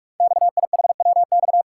Morse code, heard as a keyed tone, about 44 wpm, 690 Hz. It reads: XIHWX
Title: Code Groups - 5 Letters - 45wpm